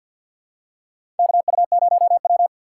Morse code, heard as a keyed tone, about 53 wpm, 700 Hz. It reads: KU0W
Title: USA Call Signs - 4 Characters - 3x Character Spacing - 50wpm